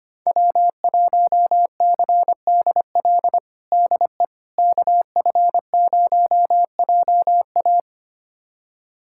Morse code, heard as a keyed tone, about 25 wpm, 700 Hz. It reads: W1CDL DE KF0JA
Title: USA Callsign DE Callsign - 5 Characters - 25wpm